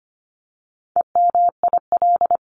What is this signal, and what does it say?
Morse code, heard as a keyed tone, about 25 wpm, 700 Hz. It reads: EMIL
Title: CWT Exchange 25wpm